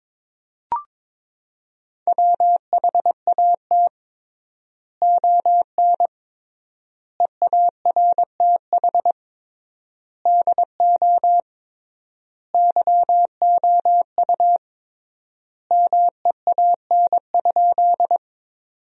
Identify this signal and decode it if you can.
Morse code, heard as a keyed tone, about 22 wpm, 700 Hz. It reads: WHAT ON EARTH DO YOU MEAN?
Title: 3x Word-Spacing - Sentences from Top 200 Words - 22wpm